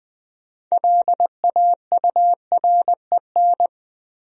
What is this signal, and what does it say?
Morse code, heard as a keyed tone, about 20 wpm, 700 Hz. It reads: LAUREN
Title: USA Common Names - 20wpm